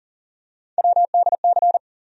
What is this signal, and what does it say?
Morse code, heard as a keyed tone, about 40 wpm, 700 Hz. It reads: WDC
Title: USA Call Sign Postfixes - 3 Letters 40wpm